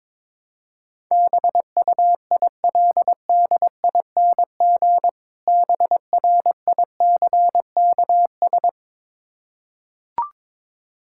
Morse code, heard as a keyed tone, about 22 wpm, 690 Hz. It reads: BUILDING BRICKS
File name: Difficult Words with Hints - 22wpm